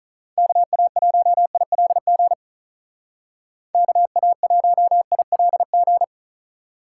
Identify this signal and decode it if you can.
Morse code, heard as a keyed tone, about 41 wpm, 700 Hz. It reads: KA1ILG KA1ILG
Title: Speed Racing - USA 2x3 Call Signs 35wpm